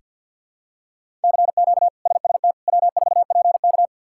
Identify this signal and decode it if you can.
Morse code, heard as a keyed tone, about 46 wpm, 700 Hz. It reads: CQ SST W4PK
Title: K1USN SST QSO Practice - 50wpm